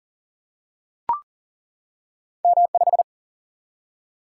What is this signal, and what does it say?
Morse code, heard as a keyed tone, about 40 wpm, 700 Hz. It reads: M5
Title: Single-Letter Single-Number 40wpm